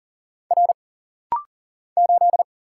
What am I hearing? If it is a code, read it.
Morse code, heard as a keyed tone, about 40 wpm, 700 Hz. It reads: R 8
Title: Single Letter-Number - Rapid-Fire 40wpm